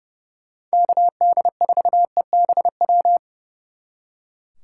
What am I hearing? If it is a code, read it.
Morse code, heard as a keyed tone, about 35 wpm, 700 Hz. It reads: KD4EBW
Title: Speed Racing - USA 2x3 Call Signs 30wpm